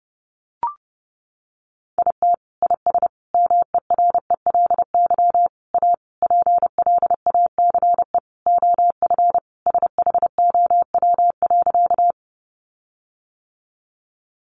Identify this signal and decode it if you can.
Morse code, heard as a keyed tone, about 30 wpm, 700 Hz. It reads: IT IS MERELY A PLACE OF SHOW.
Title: Sentences from Top 1400 Words 30wpm